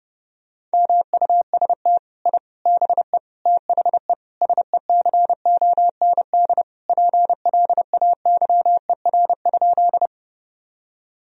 Morse code, heard as a keyed tone, about 28 wpm, 700 Hz. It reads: MUST I BE THE SECOND PLAYER?
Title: Sentences from Top 1000 Words - Review Part 4 - 30wpm